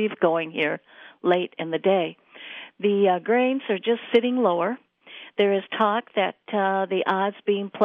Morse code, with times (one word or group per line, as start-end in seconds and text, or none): none